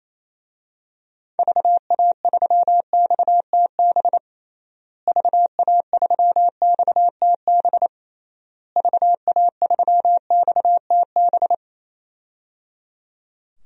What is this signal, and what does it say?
Morse code, heard as a keyed tone, about 27 wpm, 700 Hz. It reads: VA3XTB VA3XTB VA3XTB
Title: DXCC-100 Call Signs - 6 Characters - Repeated 3x - 28wpm